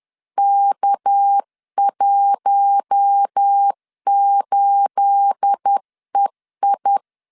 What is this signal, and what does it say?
Morse code, heard as a keyed tone, about 10 wpm, 800 Hz. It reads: K18EI
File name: LEZIONI / 9 28 10wpm 2842